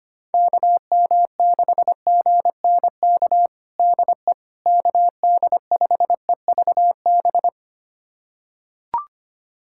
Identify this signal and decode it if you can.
Morse code, heard as a keyed tone, about 25 wpm, 700 Hz. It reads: KM6GNK DE KD5EVB
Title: USA Callsign DE Callsign - 6 Characters - 25wpm